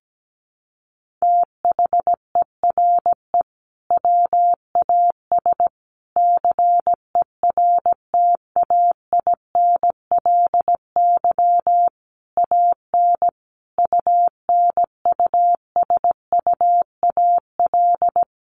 Morse code, heard as a keyed tone, about 17 wpm, 700 Hz. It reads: THERE WAS CERTAINLY AN UNUSUAL
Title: Sentences from Top 1400 Words 17wpm